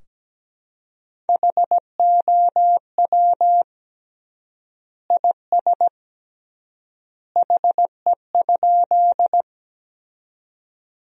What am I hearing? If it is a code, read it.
Morse code, heard as a keyed tone, about 17 wpm, 700 Hz. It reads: HOW IS HE?
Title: 3x Word-Spacing - Sentences from Top 100 Words - 17wpm